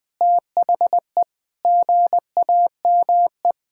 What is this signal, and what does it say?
Morse code, heard as a keyed tone, about 20 wpm, 700 Hz. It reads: THE GAME
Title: Sentences with 2-Word Phrases - 20wpm